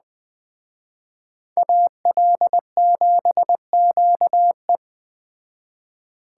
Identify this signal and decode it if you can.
Morse code, heard as a keyed tone, about 20 wpm, 700 Hz. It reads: AL7QE